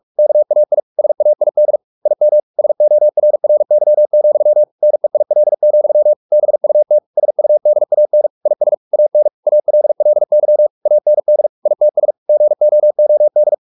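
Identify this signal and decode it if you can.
Morse code, heard as a keyed tone, about 44 wpm, 600 Hz. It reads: KAI SAID IM SORRY, NEIL, BUT SUDAN IS AN ALLY AND ITS GOOD
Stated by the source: Morse practice at 45 wpm, the novel Never by Ken Follett in English, from CW med Gnister.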